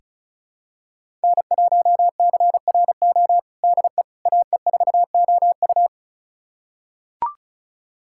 Morse code, heard as a keyed tone, about 36 wpm, 700 Hz. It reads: N1CRO DE AE4OU
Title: USA Callsign DE Callsign - 5 Characters - 35wpm